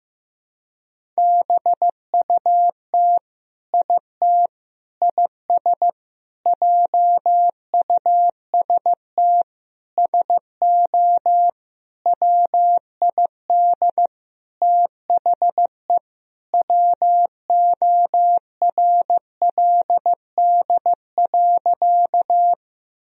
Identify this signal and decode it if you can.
Morse code, heard as a keyed tone, about 16 wpm, 700 Hz. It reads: BUT IT IS JUST SO WID THE WORLD.
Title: Sentences from Top 200 Words and Abbreviations 15wpm